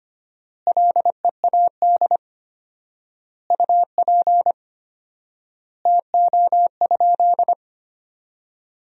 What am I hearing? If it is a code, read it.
Morse code, heard as a keyed tone, about 25 wpm, 700 Hz. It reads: LEAD UP TO?
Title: 4x Word-Spacing - Sentences from Top 500 Words - 25wpm